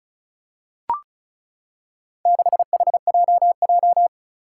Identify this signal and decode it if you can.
Morse code, heard as a keyed tone, about 36 wpm, 700 Hz. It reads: 6HJJ
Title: Four Challenging Letter-Number Combinations 35wpm